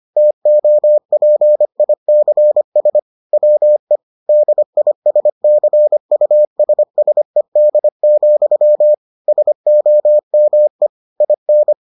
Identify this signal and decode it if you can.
Morse code, heard as a keyed tone, about 25 wpm, 600 Hz. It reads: TOPICS WE DISCUSSED, SOME IN